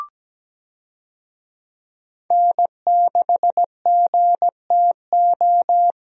Morse code, heard as a keyed tone, about 17 wpm, 700 Hz. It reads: N6GTO
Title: USA Call Signs - 5 Characters - Repeated 3x - 17wpm